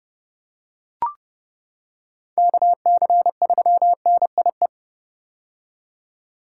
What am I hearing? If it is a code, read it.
Morse code, heard as a keyed tone, about 32 wpm, 700 Hz. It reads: KC3NIE
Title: K1USN SST QSO Practice - 30wpm